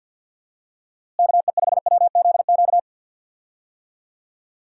Morse code, heard as a keyed tone, about 52 wpm, 700 Hz. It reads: KE5WZQ
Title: USA Call Signs - 6 Characters - Repeated 2x - 50wpm